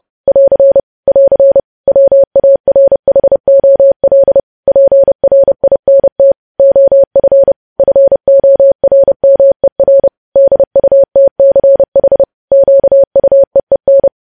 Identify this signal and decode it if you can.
Morse code, heard as a keyed tone, about 30 wpm, 550 Hz. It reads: <AR> <AR> WARHOL PRINT OF FORMER DUTCH QUEEN
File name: podcast / News headlines-1x-30x30